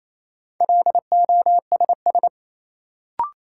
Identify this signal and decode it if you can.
Morse code, heard as a keyed tone, about 27 wpm, 700 Hz. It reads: LOSS
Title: Sentences from Top 1400 Words - New Word Drill - 28wpm